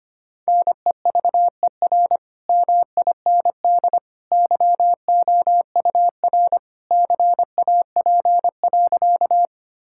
Morse code, heard as a keyed tone, about 25 wpm, 700 Hz. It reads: NEVER MIND YOUR CAP.
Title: Sentences from Top 1600 Words 25wpm